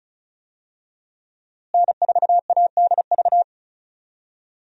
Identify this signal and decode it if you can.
Morse code, heard as a keyed tone, about 35 wpm, 700 Hz. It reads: N4ADV
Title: USA Call Signs - 5 Characters - Repeated 3x - 35wpm